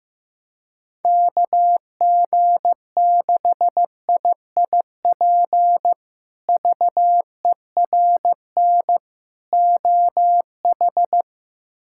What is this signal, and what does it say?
Morse code, heard as a keyed tone, about 15 wpm, 700 Hz. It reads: KG6IIP VERN OH
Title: K1USN SST QSO Practice - 15wpm